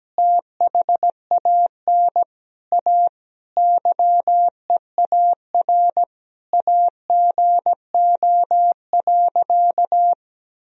Morse code, heard as a keyed tone, about 17 wpm, 700 Hz. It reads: THAN A YEAR AGO.
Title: Sentences from Top 1000 Words - Review Part 3 - 17wpm